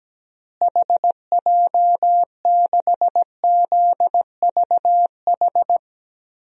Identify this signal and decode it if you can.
Morse code, heard as a keyed tone, about 17 wpm, 700 Hz. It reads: HJ6ZVH
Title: DXCC-100 Call Signs - 6 Characters - Repeated 3x - 17wpm